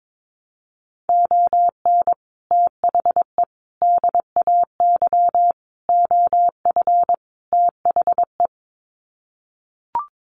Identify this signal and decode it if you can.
Morse code, heard as a keyed tone, about 22 wpm, 700 Hz. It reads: ON THE DAY OF THE
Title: Sets of 5 Words Encore 22wpm